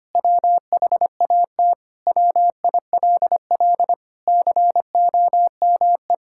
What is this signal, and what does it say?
Morse code, heard as a keyed tone, about 25 wpm, 700 Hz. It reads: WHAT WILL COME
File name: practice / 25wpm / Sentences with 2-Word Phrases - 25wpm